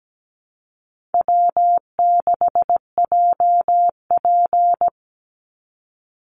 Morse code, heard as a keyed tone, about 17 wpm, 700 Hz. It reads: W6JP
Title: NA SOTA Chaser Callsigns 17wpm